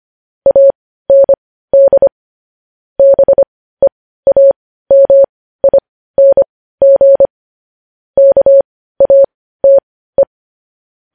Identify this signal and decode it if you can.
Morse code, heard as a keyed tone, about 25 wpm, spaced as for 15 wpm, 550 Hz. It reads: AND BEAMING KATE
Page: News headlines in Morse code at 15 wpm